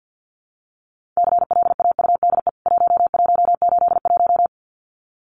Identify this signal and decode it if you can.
Morse code, heard as a keyed tone, about 48 wpm, 700 Hz. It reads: CLAUDE 1181
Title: CWT Exchange 50wpm